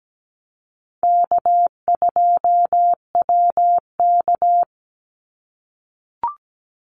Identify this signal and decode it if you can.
Morse code, heard as a keyed tone, about 17 wpm, 700 Hz. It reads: K2WK
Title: Easy Call Signs - 17wpm